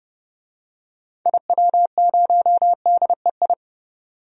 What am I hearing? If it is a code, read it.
Morse code, heard as a keyed tone, about 30 wpm, 690 Hz. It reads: IW0DEI
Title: DXCC-100 Call Signs - 6 Characters - 30wpm